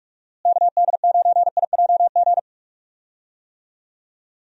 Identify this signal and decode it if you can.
Morse code, heard as a keyed tone, about 41 wpm, 700 Hz. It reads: KD9IJG
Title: USA Call Signs - 6 Characters - Repeated 3x - 45wpm